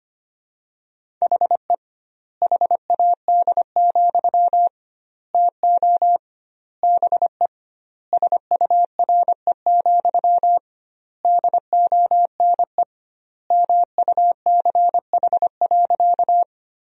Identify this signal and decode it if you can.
Morse code, heard as a keyed tone, about 25 wpm, 700 Hz. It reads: HE HAD, TO BE SURE, DONE MUCH.
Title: 2x Word-Spacing - Sentences from Top 500 Words - 25wpm